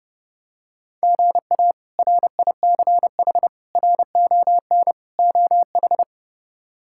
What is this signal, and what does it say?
Morse code, heard as a keyed tone, about 31 wpm, 700 Hz. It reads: GA RICH RON OH
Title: K1USN SST QSO Practice - 30wpm